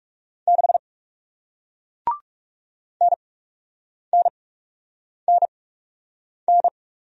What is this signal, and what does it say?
Morse code, heard as a keyed tone, about 40 wpm, spaced as for 8 wpm, 700 Hz. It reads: 6 NNNN